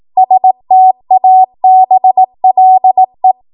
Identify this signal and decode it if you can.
Morse code, heard as a keyed tone, about 18 wpm, 750 Hz. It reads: STABLE
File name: Morse / 18WPM